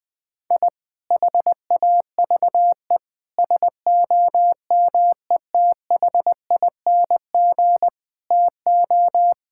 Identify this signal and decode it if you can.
Morse code, heard as a keyed tone, about 20 wpm, 700 Hz. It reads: I HAVE SOMETHING TO
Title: Sentences with 2-Word Phrases - 20wpm